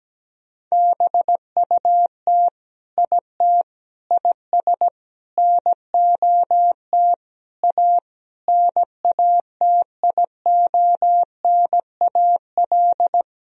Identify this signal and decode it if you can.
Morse code, heard as a keyed tone, about 17 wpm, 700 Hz. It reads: BUT IT IS NOT A NATIONAL
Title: Sentences from Top 1000 Words - Review Part 4 - 17wpm